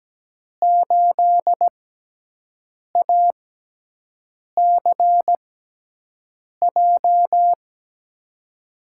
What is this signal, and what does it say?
Morse code, heard as a keyed tone, about 17 wpm, spaced as for 6 wpm, 700 Hz. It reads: 8ACJ
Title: USA Call Signs - 6 Characters - 6x Character Spacing - 17wpm